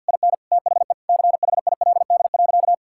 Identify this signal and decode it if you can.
Morse code, heard as a keyed tone, about 50 wpm, 700 Hz. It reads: IN THE CHILD.
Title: Sentences from Top 1000 Words - Review Part 5 - 50wpm